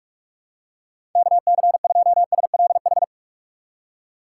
Morse code, heard as a keyed tone, about 47 wpm, 700 Hz. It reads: KC2SLH
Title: USA Call Signs - 2x Character Spacing - 45wpm